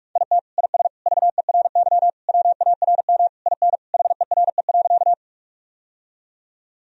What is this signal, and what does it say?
Morse code, heard as a keyed tone, about 46 wpm, 700 Hz. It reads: IT IS VERY WARM IN HERE.
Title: Sentences from Top 1000 Words - Review Part 2 - 45wpm